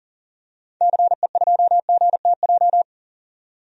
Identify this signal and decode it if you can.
Morse code, heard as a keyed tone, about 40 wpm, 700 Hz. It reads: CE2GTJ
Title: DXCC-100 Call Signs - 6 Characters - 40wpm